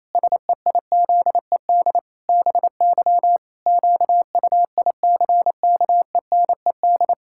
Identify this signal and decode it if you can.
Morse code, heard as a keyed tone, about 27 wpm, 700 Hz. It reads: SEIZED BY QUICKENED
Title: Pangram Challenge v2 - 28wpm